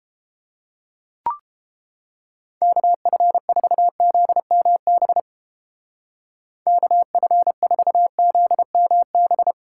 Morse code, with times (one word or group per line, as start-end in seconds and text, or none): 2.6-5.2 KF4ZMB
6.7-9.5 KF4ZMB